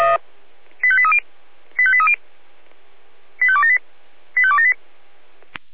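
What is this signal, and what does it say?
Five-tone selective call (ZVEI), heard as quick runs of five tones, tones of about 70 ms. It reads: 85830 85830 85278 85278
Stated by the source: ZVEI five-tone selective call received on SDR.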